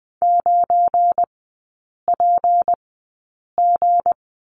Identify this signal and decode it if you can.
Morse code, heard as a keyed tone, about 20 wpm, spaced as for 8 wpm, 700 Hz. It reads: 9PG
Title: 218 Lesson 35e - Number 0 - Three Characters 20wpm